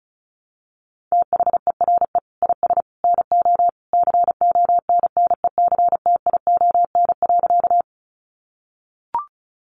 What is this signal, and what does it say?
Morse code, heard as a keyed tone, about 34 wpm, 700 Hz. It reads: THERE IS NO CONNECTION.
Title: Sentences from Top 1300 Words 35wpm